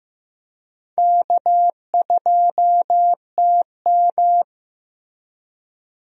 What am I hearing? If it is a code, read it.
Morse code, heard as a keyed tone, about 16 wpm, 700 Hz. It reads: K2TM
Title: USA Call Signs - 4 Characters - Repeated 2x - 15wpm